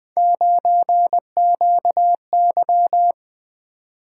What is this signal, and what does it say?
Morse code, heard as a keyed tone, about 20 wpm, 700 Hz. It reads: 9QY
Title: Three Letter-Number Combinations - 20wpm